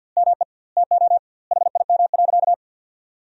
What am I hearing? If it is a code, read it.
Morse code, heard as a keyed tone, about 47 wpm, 700 Hz. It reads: ME TO HIM.
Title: 2x Word-Spacing - Sentences from Top 500 Words - 50wpm